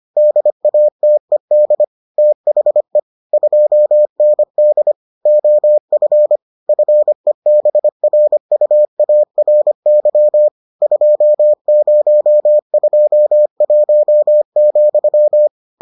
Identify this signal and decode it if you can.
Morse code, heard as a keyed tone, about 25 wpm, 600 Hz. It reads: DATED THE 2ND OF FEBRUARY 2021,